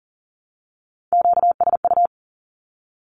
Morse code, heard as a keyed tone, about 40 wpm, 700 Hz. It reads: QSU
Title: Q-Codes Repeated 3x 40wpm